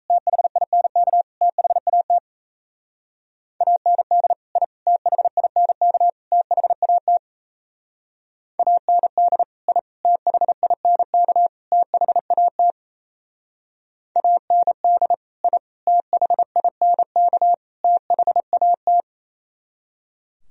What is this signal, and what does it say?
Morse code, heard as a keyed tone, about 34 wpm, 700 Hz. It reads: THINK THAT AND I THINK THAT AND I THINK THAT AND I THINK THAT
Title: Speed-Racing - Sets of 4 Words 28wpm